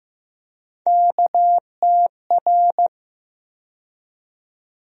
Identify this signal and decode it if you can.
Morse code, heard as a keyed tone, about 15 wpm, 700 Hz. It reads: KTR
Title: USA Call Sign Postfixes - 3 Letters 15wpm